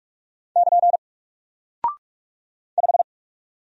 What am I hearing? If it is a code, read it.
Morse code, heard as a keyed tone, about 44 wpm, 700 Hz. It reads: <KN> 5
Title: Single Character (Letter:Number:Punctuation:Prosign) Alt 45wpm